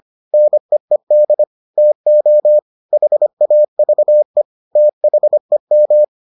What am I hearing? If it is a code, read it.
Morse code, heard as a keyed tone, about 25 wpm, 600 Hz. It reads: NEED TO HAVE THEM